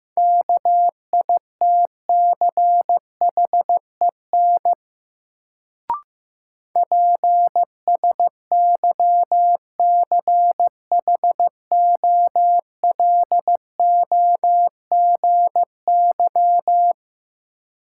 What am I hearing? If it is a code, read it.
Morse code, heard as a keyed tone, about 15 wpm, 700 Hz. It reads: KITCHEN PSYCHOLOGY
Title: Sentences from Top 1400 Words - New Word Drill - 15wpm